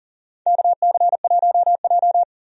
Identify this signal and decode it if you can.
Morse code, heard as a keyed tone, about 40 wpm, 700 Hz. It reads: KC1J
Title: USA Call Signs - 4 Characters - 40wpm